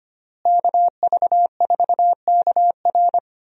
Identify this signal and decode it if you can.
Morse code, heard as a keyed tone, about 25 wpm, 700 Hz. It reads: KV4KR
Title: USA Call Signs - 5 Characters - Repeated 3x - 25wpm